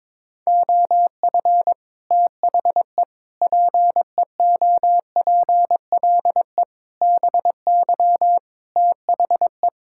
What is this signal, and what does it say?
Morse code, heard as a keyed tone, about 22 wpm, 700 Hz. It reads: OF THE PEOPLE BY THE
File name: practice / 22wpm / Sets of 8 Words - 22wpm